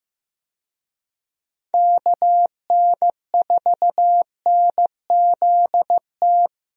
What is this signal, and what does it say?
Morse code, heard as a keyed tone, about 15 wpm, 700 Hz. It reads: KN4NZT
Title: USA Call Signs - 6 Characters - Repeated 3x - 15wpm